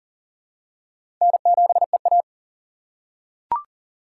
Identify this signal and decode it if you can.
Morse code, heard as a keyed tone, about 40 wpm, 700 Hz. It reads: N7EA
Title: USA Call Signs - 4 Characters - 3x Character Spacing - 40wpm